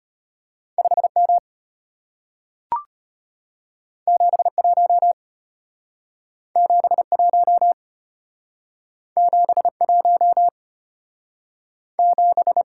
Morse code, heard as a keyed tone, about 33 wpm, 700 Hz. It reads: 5M 71 71 71 7